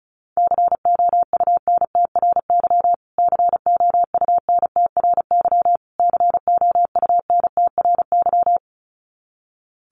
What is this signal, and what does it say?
Morse code, heard as a keyed tone, about 34 wpm, 700 Hz. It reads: COUNTRY COUNTRY COUNTRY
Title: Top 500 Words Repeated 35wpm